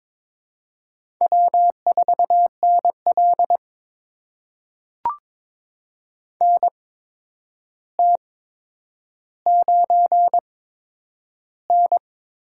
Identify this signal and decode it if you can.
Morse code, heard as a keyed tone, about 22 wpm, spaced as for 9 wpm, 700 Hz. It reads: W4NL NT9N
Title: USA Call Signs - 4 Characters - 8x Character Spacing - 22wpm